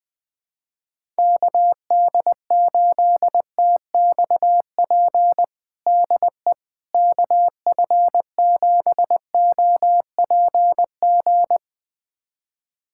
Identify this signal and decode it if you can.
Morse code, heard as a keyed tone, about 21 wpm, 700 Hz. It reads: KD8TXP DE KF7OPG
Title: USA Callsign DE Callsign - 6 Characters - 20wpm